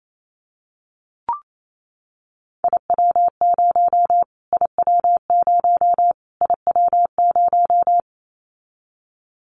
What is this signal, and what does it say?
Morse code, heard as a keyed tone, about 28 wpm, 700 Hz. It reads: IW0 IW0 IW0
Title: Callsign Prefixes for Top 25 DXCC - Repeated 3x - 28wpm